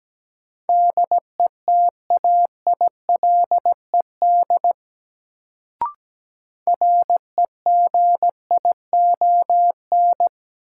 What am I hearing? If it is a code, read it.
Morse code, heard as a keyed tone, about 17 wpm, 700 Hz. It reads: DETAILED REGION
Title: Sentences from Top 1500 Words - New Word Drill Alt - 17wpm